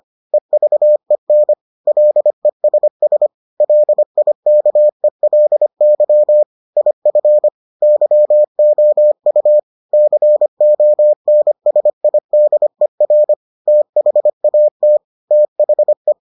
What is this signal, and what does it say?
Morse code, heard as a keyed tone, about 25 wpm, 600 Hz. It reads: EVEN LESS LIKELY IF YOU CONSIDER THAT THE